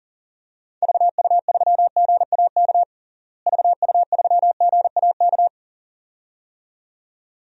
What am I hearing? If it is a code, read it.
Morse code, heard as a keyed tone, about 40 wpm, 700 Hz. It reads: VU3GAK VU3GAK
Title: DXCC-100 Call Signs - 6 Characters - Repeated 2x - 40wpm